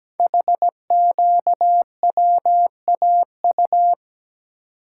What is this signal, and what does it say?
Morse code, heard as a keyed tone, about 17 wpm, 700 Hz. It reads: HQWAU
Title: Code Groups - 5 Letters - 17wpm